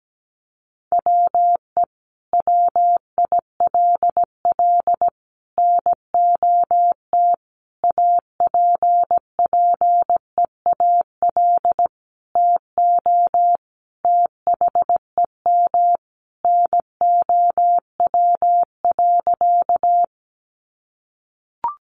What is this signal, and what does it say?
Morse code, heard as a keyed tone, about 17 wpm, 700 Hz. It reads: WE WILL NOT APPEAL TO THEM NOW.
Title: Sentences from Top 1300 Words 17wpm